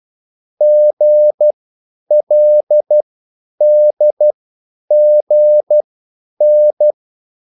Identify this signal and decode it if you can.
Morse code, heard as a keyed tone, about 12 wpm, spaced as for 9 wpm, 600 Hz. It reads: GLDGN
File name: QSL / CW lesson